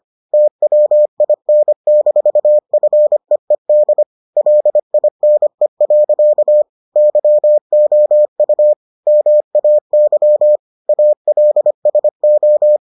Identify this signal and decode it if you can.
Morse code, heard as a keyed tone, about 25 wpm, 600 Hz. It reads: TWIN-FEED LINE. YOU MAY ALSO